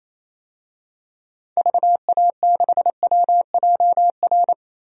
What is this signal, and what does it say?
Morse code, heard as a keyed tone, about 27 wpm, 700 Hz. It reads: VA6WJR
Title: DXCC-100 Call Signs - 6 Characters - Repeated 2x - 28wpm